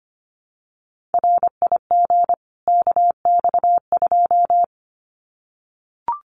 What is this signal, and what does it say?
Morse code, heard as a keyed tone, about 25 wpm, 700 Hz. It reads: RIG KX2
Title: Common QSO Phrases 25wpm